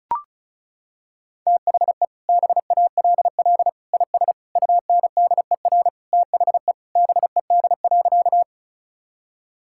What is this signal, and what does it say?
Morse code, heard as a keyed tone, about 36 wpm, 700 Hz. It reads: THE BALL IS UNDER THE BED.
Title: Sentences from Top 1000 Words - Review Part 6 - 35wpm